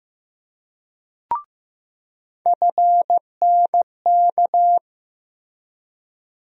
Morse code, heard as a keyed tone, about 15 wpm, 700 Hz. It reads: FNK
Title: USA Call Sign Postfixes - 3 Letters 15wpm